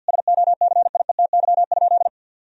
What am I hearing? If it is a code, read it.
Morse code, heard as a keyed tone, about 50 wpm, 700 Hz. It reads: SOCIETY?
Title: Sentences from Top 1000 Words - Review Part 4 - 50wpm